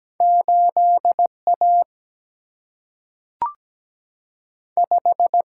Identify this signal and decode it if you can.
Morse code, heard as a keyed tone, about 17 wpm, 700 Hz. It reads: 8A 5
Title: Single-Letter Single-Number 17wpm